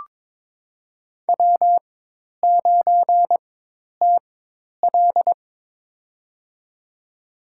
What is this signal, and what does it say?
Morse code, heard as a keyed tone, about 22 wpm, spaced as for 10 wpm, 700 Hz. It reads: W9TL
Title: USA Call Signs - 4 Characters - 4x Character Spacing - 22wpm